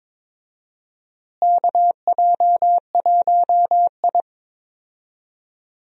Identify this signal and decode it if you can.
Morse code, heard as a keyed tone, about 22 wpm, 700 Hz. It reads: KJ1I
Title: USA Call Signs - 4 Characters - Repeated 3x - 22wpm